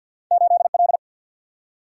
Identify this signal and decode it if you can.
Morse code, heard as a keyed tone, about 51 wpm, 700 Hz. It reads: 8 L
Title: Single-Letter Single-Number 50wpm